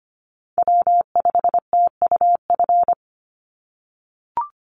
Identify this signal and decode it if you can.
Morse code, heard as a keyed tone, about 25 wpm, 700 Hz. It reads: W5TUF
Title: Call Signs - 25wpm